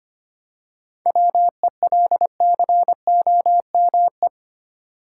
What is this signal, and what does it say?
Morse code, heard as a keyed tone, about 25 wpm, 700 Hz. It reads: WELCOME
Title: Seven Letter Words - 25wpm